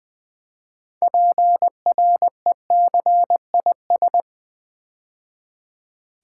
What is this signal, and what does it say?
Morse code, heard as a keyed tone, about 20 wpm, 700 Hz. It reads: PRECIS
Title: Build A Word - 3 Letters At A Time - 9-Letter Words - 20wpm